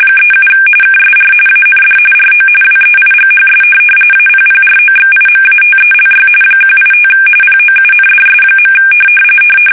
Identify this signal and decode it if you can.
RTTY, about 45.45 baud. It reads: 37 KSR AND RO WEIGHS APPROXIMATELY 185 POUNDS.
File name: fsk850 / teletype-corp